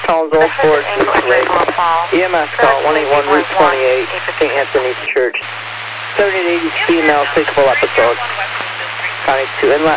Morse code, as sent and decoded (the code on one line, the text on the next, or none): none